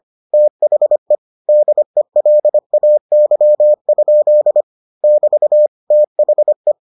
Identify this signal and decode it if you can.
Morse code, heard as a keyed tone, about 25 wpm, 600 Hz. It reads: THE DELAY? <BT> THE